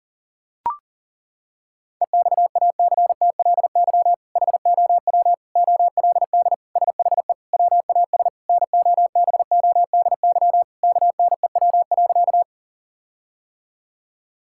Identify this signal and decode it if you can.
Morse code, heard as a keyed tone, about 37 wpm, 700 Hz. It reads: EXACTLY HOW OLD SHE WAS NOBODY KNEW.
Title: Sentences from Top 1600 Words 40wpm